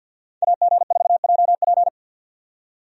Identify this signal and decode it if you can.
Morse code, heard as a keyed tone, about 50 wpm, 700 Hz. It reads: AG4JP
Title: USA Call Signs - 5 Characters - 50wpm